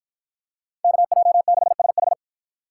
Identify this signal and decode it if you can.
Morse code, heard as a keyed tone, about 51 wpm, 700 Hz. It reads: KJ6SH